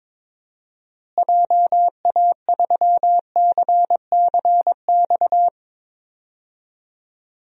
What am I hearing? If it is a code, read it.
Morse code, heard as a keyed tone, about 22 wpm, 700 Hz. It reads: JA3CCX
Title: DXCC-100 Call Signs - 6 Characters - Repeated 3x - 22wpm